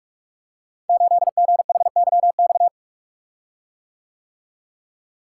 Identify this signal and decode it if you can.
Morse code, heard as a keyed tone, about 44 wpm, 700 Hz. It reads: 8GHYX
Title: Code Groups - 5 Letter:Numbers - 45wpm